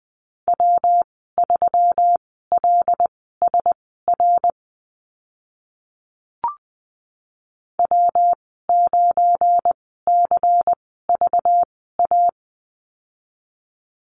Morse code, heard as a keyed tone, about 20 wpm, spaced as for 14 wpm, 700 Hz. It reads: W3LSR W9CVA
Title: 193 Lesson 31g - Letter v - Call Signs 20wpm